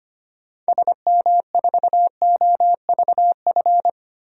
Morse code, heard as a keyed tone, about 25 wpm, 700 Hz. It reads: SM4OVF
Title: DXCC-100 Call Signs - 6 Characters - 25wpm